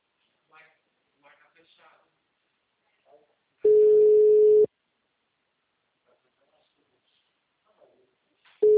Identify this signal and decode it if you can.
Ringback tone (Europe, 425 Hz), with speech.